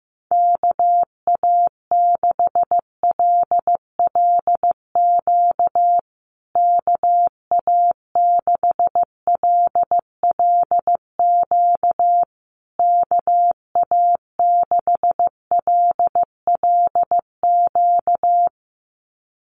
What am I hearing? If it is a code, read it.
Morse code, heard as a keyed tone, about 15 wpm, 700 Hz. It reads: KA6LLQ KA6LLQ KA6LLQ
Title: USA Call Signs Repeated 3x 15wpm